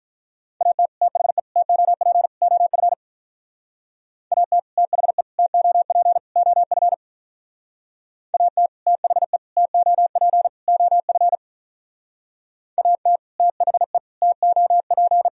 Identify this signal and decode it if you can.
Morse code, heard as a keyed tone, about 45 wpm, 700 Hz. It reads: AT THE TOP OF AT THE TOP OF AT THE TOP OF AT THE TOP